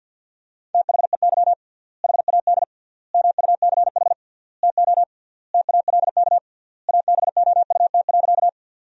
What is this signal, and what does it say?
Morse code, heard as a keyed tone, about 50 wpm, 700 Hz. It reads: THEY HAD MUCH TO TALK ABOUT.
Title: 3x Word-Spacing - Sentences from Top 400 Words - 50wpm